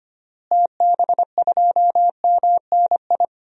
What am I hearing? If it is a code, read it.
Morse code, heard as a keyed tone, about 25 wpm, 700 Hz. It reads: TB2MNI
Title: DXCC-100 Call Signs - 6 Characters - Repeated 2x - 25wpm